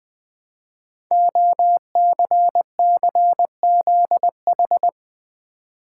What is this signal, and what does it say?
Morse code, heard as a keyed tone, about 20 wpm, 700 Hz. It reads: OCCZH
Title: Code Groups - 5 Letters - 20wpm